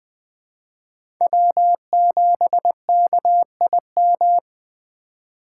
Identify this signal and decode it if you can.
Morse code, heard as a keyed tone, about 20 wpm, 700 Hz. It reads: W7KIM
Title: USA Call Signs - 5 Characters - 4x Character Spacing - 20wpm